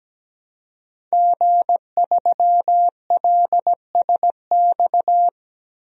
Morse code, heard as a keyed tone, about 17 wpm, 700 Hz. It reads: G3LSX
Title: DXCC-100 Call Signs - 5 Characters - 17wpm